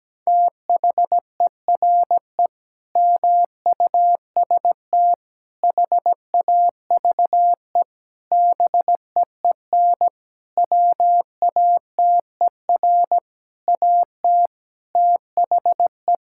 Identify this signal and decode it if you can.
Morse code, heard as a keyed tone, about 17 wpm, 700 Hz. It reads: THERE MUST HAVE BEEN WATER AT THE
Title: Sentences from Top 1000 Words - Review Part 6 - 17wpm